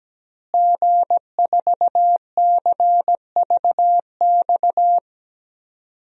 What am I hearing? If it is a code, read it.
Morse code, heard as a keyed tone, about 17 wpm, 700 Hz. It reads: G4CVX
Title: DXCC-100 Call Signs - 5 Characters - Repeated 2x - 17wpm